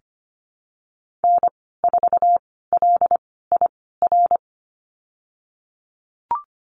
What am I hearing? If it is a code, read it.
Morse code, heard as a keyed tone, about 25 wpm, spaced as for 16 wpm, 700 Hz. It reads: N4LIR
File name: Lesson 15g - Number 5 - Call Signs 25wpm